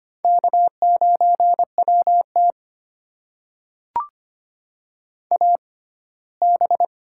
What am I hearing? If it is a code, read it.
Morse code, heard as a keyed tone, about 25 wpm, spaced as for 17 wpm, 700 Hz. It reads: K9WT AB
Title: USA Call Signs - 4 Characters - 6x Character Spacing - 25wpm